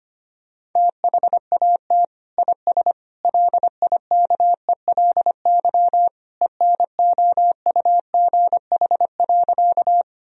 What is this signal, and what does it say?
Morse code, heard as a keyed tone, about 25 wpm, 700 Hz. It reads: THAT IS LIKELY ENOUGH.
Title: Sentences from Top 1000 Words - Review Part 4 - 25wpm